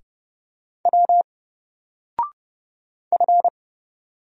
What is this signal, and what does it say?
Morse code, heard as a keyed tone, about 29 wpm, 700 Hz. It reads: W F
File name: Single-Letter-Number-30wpm